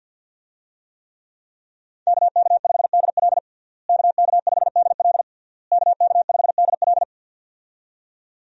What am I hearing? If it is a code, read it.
Morse code, heard as a keyed tone, about 52 wpm, 700 Hz. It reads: KK5DL KK5DL KK5DL
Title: USA Call Signs - 5 Characters - Repeated 3x - 50wpm